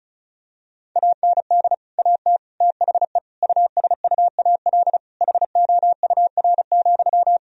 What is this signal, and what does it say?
Morse code, heard as a keyed tone, about 36 wpm, 700 Hz. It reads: AND AT THE USUAL HOUR,
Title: Sentences from Top 1000 Words - Review Part 2 - 35wpm